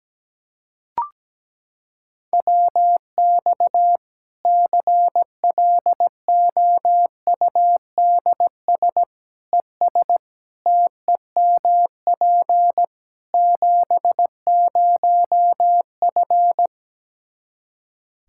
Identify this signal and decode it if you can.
Morse code, heard as a keyed tone, about 17 wpm, 700 Hz. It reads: WX CLOUDS ES TEMP 70F
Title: Practice QSO - Exchange 2 - Info - 17wpm